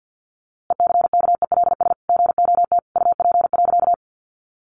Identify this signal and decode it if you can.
Morse code, heard as a keyed tone, about 50 wpm, 700 Hz. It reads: ECKELS GOT UP.
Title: A Sound Of Thunder 50wpm